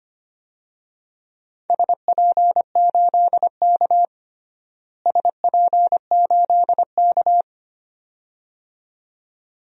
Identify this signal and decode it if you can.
Morse code, heard as a keyed tone, about 25 wpm, 700 Hz. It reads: SP8K SP8K
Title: DXCC-100 Call Signs - 4 Characters - Repeated 2x - 25wpm